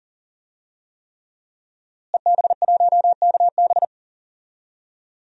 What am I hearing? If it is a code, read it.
Morse code, heard as a keyed tone, about 40 wpm, 700 Hz. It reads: EB1KB
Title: DXCC-100 Call Signs - 5 Characters - Repeated 3x - 40wpm